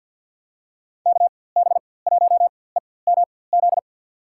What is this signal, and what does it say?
Morse code, heard as a keyed tone, about 47 wpm, spaced as for 23 wpm, 700 Hz. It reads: KB1EMZ
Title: USA Call Signs - 6 Characters - 4x Character Spacing - 50wpm